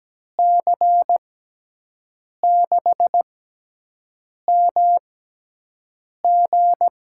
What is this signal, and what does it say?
Morse code, heard as a keyed tone, about 17 wpm, spaced as for 6 wpm, 700 Hz. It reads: C6MG
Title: USA Call Signs - 5 Characters - 6x Character Spacing - 17wpm